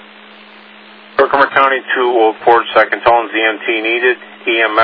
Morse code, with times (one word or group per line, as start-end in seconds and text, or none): none